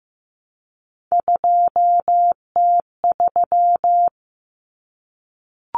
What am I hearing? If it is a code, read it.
Morse code, heard as a keyed tone, about 15 wpm, 700 Hz. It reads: 2T3
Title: Common Cut Numbers - 3-Digit 15wpm